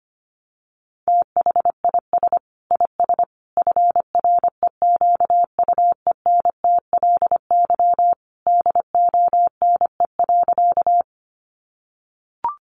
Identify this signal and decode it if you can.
Morse code, heard as a keyed tone, about 25 wpm, 700 Hz. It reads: THIS IS FREQUENTLY DONE.
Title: Sentences from Top 1300 Words 25wpm